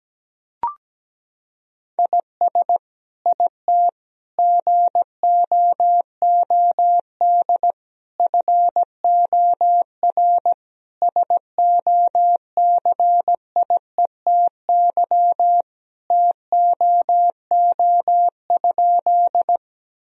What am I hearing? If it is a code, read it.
Morse code, heard as a keyed tone, about 17 wpm, 700 Hz. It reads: IS IT GOOD FOR SOCIETY TOO?
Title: Sentences from Top 1000 Words - Review Part 4 - 17wpm